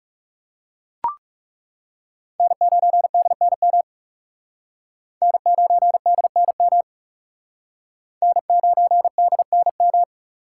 Morse code, heard as a keyed tone, about 40 wpm, 700 Hz. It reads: N9DNM N9DNM N9DNM